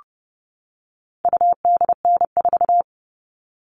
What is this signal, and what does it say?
Morse code, heard as a keyed tone, about 30 wpm, 700 Hz. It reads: UDN4
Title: All Cut Numbers - 4-Digit 30wpm